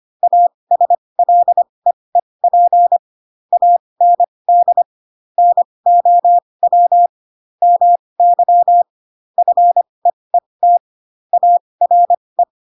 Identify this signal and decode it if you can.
Morse code, heard as a keyed tone, about 25 wpm, spaced as for 20 wpm, 700 Hz. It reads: ASLEEP AND NOW MY FEET ARE
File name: Pinocchio / Pinocchio ch7 15 25 ik1hge